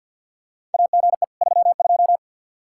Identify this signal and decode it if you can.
Morse code, heard as a keyed tone, about 48 wpm, 700 Hz. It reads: AGE 32
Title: Practice QSO - Exchange 3 - More Info - 50wpm